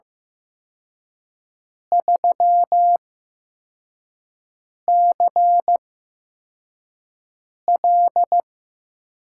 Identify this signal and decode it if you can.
Morse code, heard as a keyed tone, about 15 wpm, spaced as for 4 wpm, 700 Hz. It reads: 3CL
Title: USA Call Signs - 4 Characters - 8x Character Spacing - 15wpm